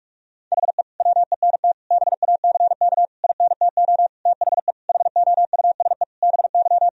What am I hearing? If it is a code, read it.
Morse code, heard as a keyed tone, about 46 wpm, 700 Hz. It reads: HE WENT BACK INTO THE HOUSE BY
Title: Sentences from Top 1000 Words - Review Part 2 - 45wpm